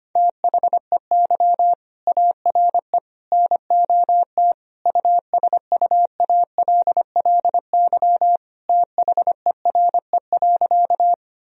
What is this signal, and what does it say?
Morse code, heard as a keyed tone, about 25 wpm, 700 Hz. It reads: THEY ARE NOT USUALLY THERE.
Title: Sentences from Top 1000 Words - Review Part 3 - 25wpm